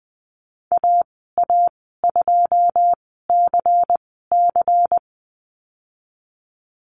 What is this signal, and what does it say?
Morse code, heard as a keyed tone, about 20 wpm, spaced as for 15 wpm, 700 Hz. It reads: AA2CC
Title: 154 Lesson 25g - Letter p - Call Signs 20wpm